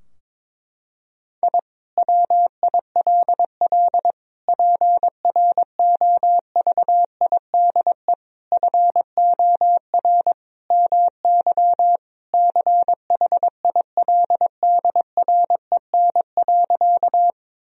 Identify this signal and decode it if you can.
Morse code, heard as a keyed tone, about 22 wpm, 700 Hz. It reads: I WILL PROVIDE FOR MY CHILDREN.
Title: Sentences from Top 1000 Words - Review Part 3 - 22wpm